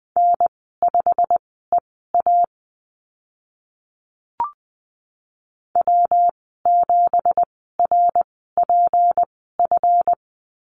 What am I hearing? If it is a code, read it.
Morse code, heard as a keyed tone, about 20 wpm, spaced as for 14 wpm, 700 Hz. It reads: N5EA W7RPF
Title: 173 Lesson 28g - Number 9 - Call Signs 20wpm